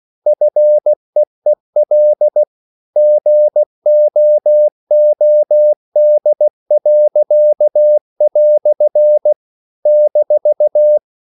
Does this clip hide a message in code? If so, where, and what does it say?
Yes: Morse code from 0.3 to 11.0 s, heard as a keyed tone, about 16 wpm, 600 Hz. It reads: FEEL GOOD." -